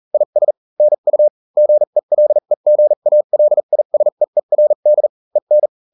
Morse code, heard as a keyed tone, about 40 wpm, spaced as for 35 wpm, 600 Hz. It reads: IS NU GELEGALISEERD EN